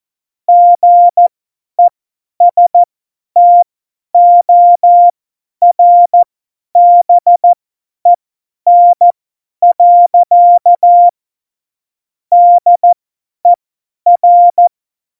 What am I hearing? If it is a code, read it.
Morse code, heard as a keyed tone, about 14 wpm, spaced as for 10 wpm, 700 Hz. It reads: GESTORBEN. DER
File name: Morsekurs / Witze / Witz 01. September 2020, 50 F70